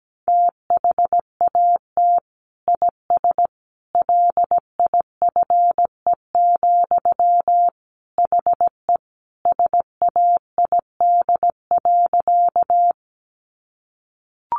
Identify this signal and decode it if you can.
Morse code, heard as a keyed tone, about 17 wpm, 700 Hz. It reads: THAT IS LIFE, HE SAID.
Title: Sentences from Top 900 Words 17wpm